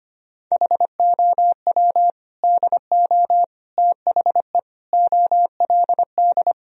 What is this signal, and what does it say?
Morse code, heard as a keyed tone, about 25 wpm, 700 Hz. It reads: HOW DO THE OLD
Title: Sentences from Top 1000 Words - Review Part 1 - 25wpm